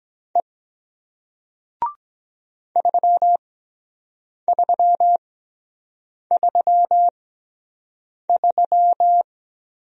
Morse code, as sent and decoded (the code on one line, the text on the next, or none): . / ...-- ...-- ...-- ...--
E 3333